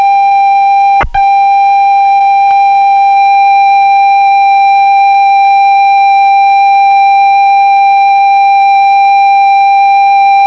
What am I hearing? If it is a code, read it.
Computer data on cassette.